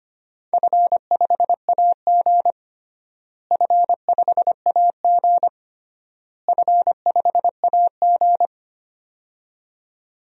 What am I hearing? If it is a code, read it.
Morse code, heard as a keyed tone, about 25 wpm, 700 Hz. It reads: F5AG F5AG F5AG
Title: DXCC-100 Call Signs - 4 Characters - Repeated 3x - 25wpm